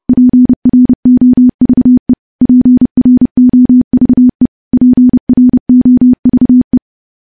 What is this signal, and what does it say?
Morse code, heard as a keyed tone, about 30 wpm, 250 Hz. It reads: PROVE PROVE PROVE
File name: morse-training-30WPM